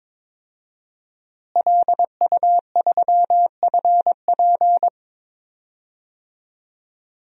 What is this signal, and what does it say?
Morse code, heard as a keyed tone, about 23 wpm, 700 Hz. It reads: LU3FP
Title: DXCC-100 Call Signs - 5 Characters - Repeated 2x - 22wpm